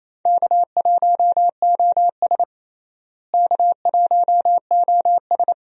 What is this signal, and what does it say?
Morse code, heard as a keyed tone, about 28 wpm, 700 Hz. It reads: K1OS K1OS
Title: USA Call Signs - 4 Characters - Repeated 3x - 28wpm